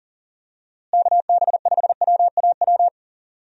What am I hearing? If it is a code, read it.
Morse code, heard as a keyed tone, about 37 wpm, 700 Hz. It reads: KB5WAW
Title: USA Call Signs - 6 Characters - Repeated 3x - 40wpm